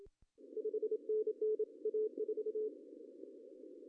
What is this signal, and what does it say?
Morse code, heard as a keyed tone, about 30 wpm, 430 Hz. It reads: 5NN A4
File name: CW / cqww-cw-k5tr